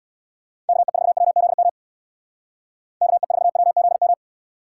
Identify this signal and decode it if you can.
Morse code, heard as a keyed tone, about 64 wpm, 700 Hz. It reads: KE4WZM KE4WZM